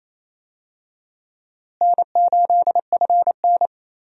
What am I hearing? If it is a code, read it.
Morse code, heard as a keyed tone, about 28 wpm, 700 Hz. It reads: N8FN
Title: USA Call Signs - 4 Characters - Repeated 2x - 28wpm